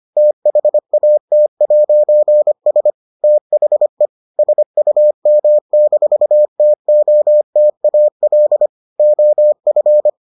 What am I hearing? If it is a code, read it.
Morse code, heard as a keyed tone, about 25 wpm, 600 Hz. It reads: THAT'S THE SUM-TOTAL OF